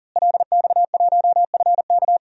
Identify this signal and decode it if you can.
Morse code, heard as a keyed tone, about 40 wpm, 700 Hz. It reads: LX1FK
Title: DXCC-100 Call Signs - 5 Characters - Repeated 2x - 40wpm